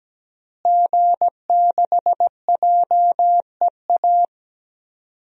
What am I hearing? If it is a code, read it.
Morse code, heard as a keyed tone, about 17 wpm, 700 Hz. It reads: G6JEA
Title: DXCC-100 Call Signs - 5 Characters - Repeated 3x - 17wpm